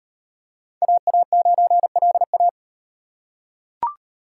Morse code, heard as a keyed tone, about 34 wpm, 700 Hz. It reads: AA9LA